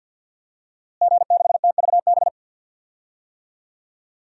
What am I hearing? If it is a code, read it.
Morse code, heard as a keyed tone, about 46 wpm, 700 Hz. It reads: G6TVB
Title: DXCC-100 Call Signs - 5 Characters - Repeated 3x - 50wpm